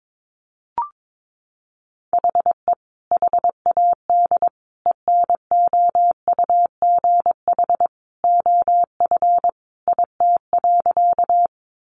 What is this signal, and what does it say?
Morse code, heard as a keyed tone, about 22 wpm, 700 Hz. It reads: HE HAD ENOUGH OF IT.
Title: Sentences from Top 400 Words 22wpm